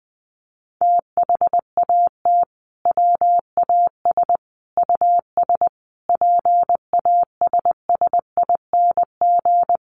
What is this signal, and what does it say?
Morse code, heard as a keyed tone, about 20 wpm, 700 Hz. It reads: THAT WAS US PASSING
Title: A Sound Of Thunder 20wpm